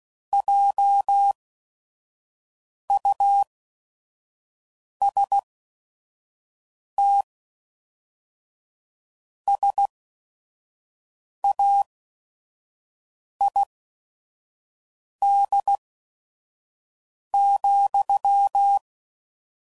Morse code, heard as a keyed tone, about 16 wpm, spaced as for 5 wpm, 800 Hz. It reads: JUST SAID,
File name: CW Code Groups @5WPM